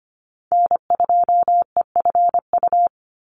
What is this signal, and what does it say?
Morse code, heard as a keyed tone, about 25 wpm, 700 Hz. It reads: N2EFU
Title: USA Call Signs 25wpm